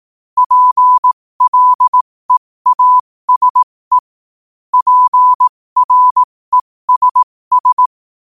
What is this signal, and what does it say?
Morse code, heard as a keyed tone, about 18 wpm, spaced as for 15 wpm, 1000 Hz. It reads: PLEASE PRESS
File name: qotdfiles / 15wpm